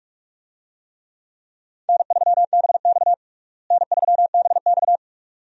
Morse code, heard as a keyed tone, about 47 wpm, 700 Hz. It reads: N3BX N3BX
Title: USA Call Signs - 4 Characters - Repeated 2x - 45wpm